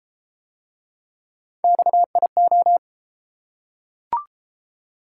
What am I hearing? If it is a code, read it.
Morse code, heard as a keyed tone, about 30 wpm, 700 Hz. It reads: XIO